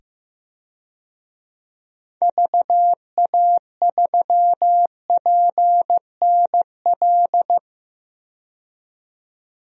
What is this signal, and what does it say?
Morse code, heard as a keyed tone, about 16 wpm, 700 Hz. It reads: VA3PNL
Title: DXCC-100 Call Signs - 6 Characters - 15wpm